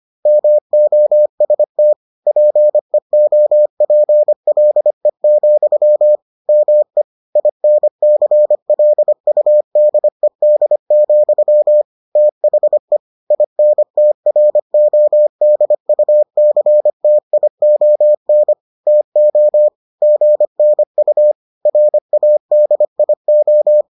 Morse code, heard as a keyed tone, about 25 wpm, 600 Hz. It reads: MOST PEOPLE, ME INCLUDED, THE INTRODUCTION TO GNU RADIO